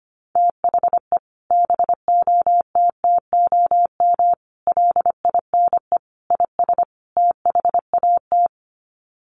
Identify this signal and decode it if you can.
Morse code, heard as a keyed tone, about 25 wpm, 700 Hz. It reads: THE BOTTOM LINE IS THAT
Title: Sets of 5 Words Encore 25wpm